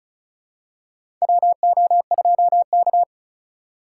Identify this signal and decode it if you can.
Morse code, heard as a keyed tone, about 34 wpm, 700 Hz. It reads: WO2K
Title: USA Call Signs - 4 Characters - 2x Character Spacing - 35wpm